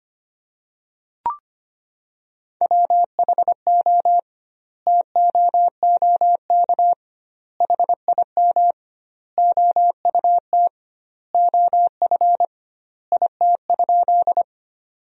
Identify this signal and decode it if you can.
Morse code, heard as a keyed tone, about 25 wpm, 700 Hz. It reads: WHO TOOK HIM OUT OF IT?
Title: 2x Word-Spacing - Sentences from Top 300 Words - 25wpm